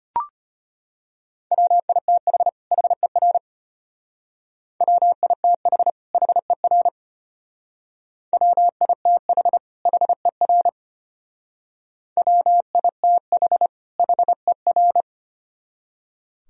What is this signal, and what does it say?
Morse code, heard as a keyed tone, about 33 wpm, 700 Hz. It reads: WITH HER WITH HER WITH HER WITH HER